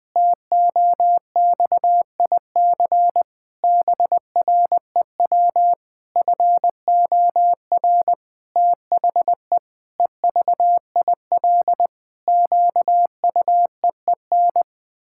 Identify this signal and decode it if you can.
Morse code, heard as a keyed tone, about 20 wpm, 700 Hz. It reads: TOXIC BREW FOR THE EVIL QUEEN
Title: Pangram Challenge v2 - 20wpm